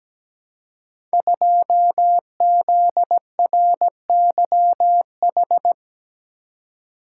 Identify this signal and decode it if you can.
Morse code, heard as a keyed tone, about 17 wpm, 700 Hz. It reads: 2ZRYH
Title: Code Groups - 5 Letter:Numbers - 17wpm